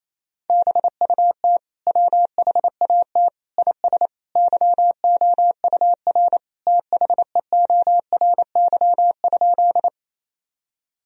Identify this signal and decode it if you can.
Morse code, heard as a keyed tone, about 28 wpm, 700 Hz. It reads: BUT WHAT IS YOUR THEORY?
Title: Sentences from Top 1000 Words - Review Part 3 - 28wpm